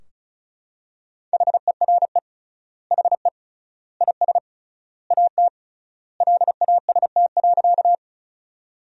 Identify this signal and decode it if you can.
Morse code, heard as a keyed tone, about 36 wpm, 700 Hz. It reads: HERE HE IS AT LAST.
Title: 3x Word-Spacing - Sentences from Top 300 Words - 35wpm